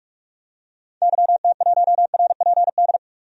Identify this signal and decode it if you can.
Morse code, heard as a keyed tone, about 46 wpm, 700 Hz. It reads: YT1RPD
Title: DXCC-100 Call Signs - 6 Characters - 45wpm